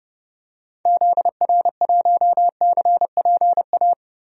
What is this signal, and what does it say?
Morse code, heard as a keyed tone, about 30 wpm, 700 Hz. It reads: ZR1CPA
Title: DXCC-100 Call Signs - 6 Characters - 30wpm